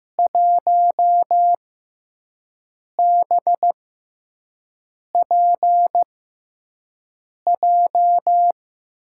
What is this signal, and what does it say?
Morse code, heard as a keyed tone, about 15 wpm, spaced as for 5 wpm, 700 Hz. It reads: 1BPJ
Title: USA Call Signs - 5 Characters - 6x Character Spacing - 15wpm